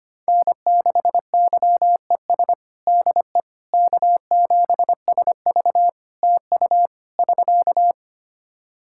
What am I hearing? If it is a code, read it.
Morse code, heard as a keyed tone, about 25 wpm, 700 Hz. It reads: N6YES DE K7SV TU <SK>
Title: K1USN SST QSO Practice 25wpm